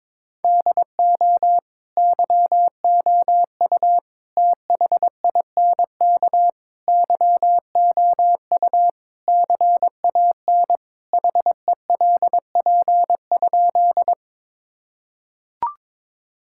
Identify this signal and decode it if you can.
Morse code, heard as a keyed tone, about 22 wpm, 700 Hz. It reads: DO YOU THINK YOU CAN HELP?
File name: practice / 22wpm / Sentences with 4-Word Phrases - 22wpm